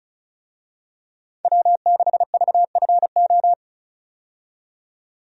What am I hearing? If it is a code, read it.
Morse code, heard as a keyed tone, about 36 wpm, 700 Hz. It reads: W6VFO
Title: USA Call Signs - 5 Characters - 35wpm